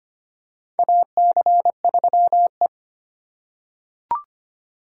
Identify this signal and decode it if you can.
Morse code, heard as a keyed tone, about 25 wpm, 700 Hz. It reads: AC3E
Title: USA Call Signs - 4x Character Spacing - 25wpm